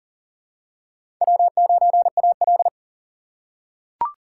Morse code, heard as a keyed tone, about 40 wpm, 700 Hz. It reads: W9AL
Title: USA Call Signs - 4x Character Spacing - 40wpm